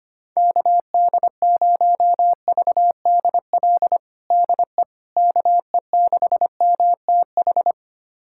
Morse code, heard as a keyed tone, about 25 wpm, 700 Hz. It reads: KD0VDL DE KE6MTH
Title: USA Callsign DE Callsign - 6 Characters - 25wpm